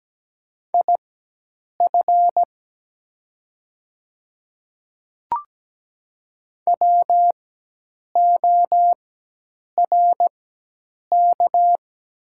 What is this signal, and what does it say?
Morse code, heard as a keyed tone, about 17 wpm, spaced as for 8 wpm, 700 Hz. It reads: IF WORK
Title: 4x Character Spacing - Top 200 Words - 17wpm